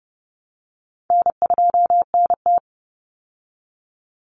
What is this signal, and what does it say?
Morse code, heard as a keyed tone, about 30 wpm, 700 Hz. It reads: N2NT
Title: Easy Call Signs Repeated 2x - 30wpm